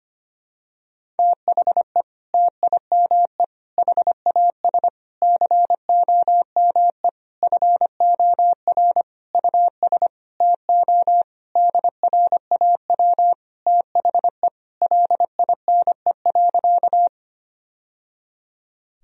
Morse code, heard as a keyed tone, about 25 wpm, 700 Hz. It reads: THE TIME HAS COME FOR US TO DRAW THE LINE.
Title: Sentences from Top 1000 Words - Review Part 1 - 25wpm